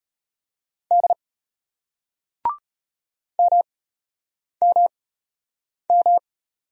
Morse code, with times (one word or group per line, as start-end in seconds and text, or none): 0.9-1.1 D
3.4-6.2 MMM